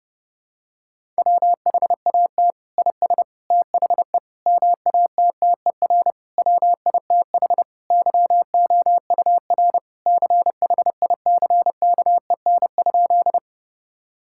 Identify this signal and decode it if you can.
Morse code, heard as a keyed tone, about 30 wpm, 700 Hz. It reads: WHAT IS THE MATTER WITH YOUR CHICKEN?
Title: Sentences from Top 1000 Words - Review Part 5 - 30wpm